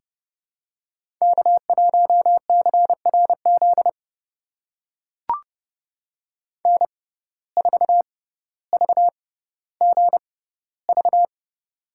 Morse code, heard as a keyed tone, about 30 wpm, spaced as for 15 wpm, 700 Hz. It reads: K1CRZ N4VGV
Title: USA Call Signs - 5 Characters - 6x Character Spacing - 30wpm